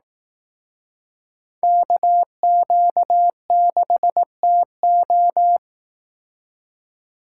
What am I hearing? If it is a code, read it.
Morse code, heard as a keyed tone, about 18 wpm, 700 Hz. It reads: KQ6TO
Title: Speed Racing - USA 2x2 Call Signs 15wpm